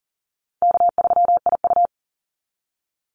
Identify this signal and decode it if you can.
Morse code, heard as a keyed tone, about 40 wpm, 700 Hz. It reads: K3IU
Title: Call Signs Repeated 2x - 40wpm